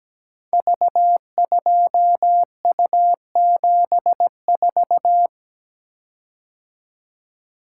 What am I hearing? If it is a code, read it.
Morse code, heard as a keyed tone, about 17 wpm, 700 Hz. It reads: V2U74
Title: Code Groups - 5 Letter:Numbers - 17wpm